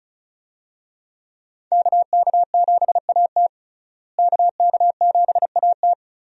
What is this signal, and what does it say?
Morse code, heard as a keyed tone, about 36 wpm, 700 Hz. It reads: KK7AT KK7AT
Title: USA Call Signs - 5 Characters - Repeated 3x - 35wpm